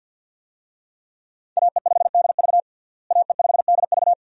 Morse code, heard as a keyed tone, about 50 wpm, 700 Hz. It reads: AE5DV AE5DV
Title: USA Call Signs - 5 Characters - Repeated 2x - 50wpm